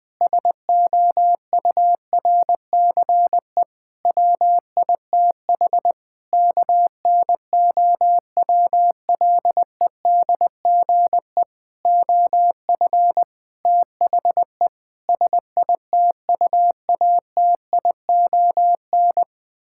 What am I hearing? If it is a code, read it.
Morse code, heard as a keyed tone, about 21 wpm, 700 Hz. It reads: SOURCE WITH KNOWLEDGE OF THE SITUATION
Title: Sentences with 9-Word Phrases - 20wpm